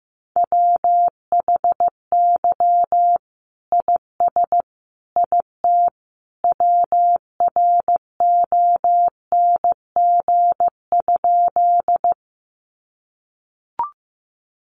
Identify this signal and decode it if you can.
Morse code, heard as a keyed tone, about 15 wpm, 700 Hz. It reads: WHY IS IT WRONG?
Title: Sentences from Top 900 Words 15wpm